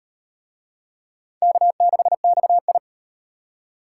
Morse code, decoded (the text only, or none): K6XI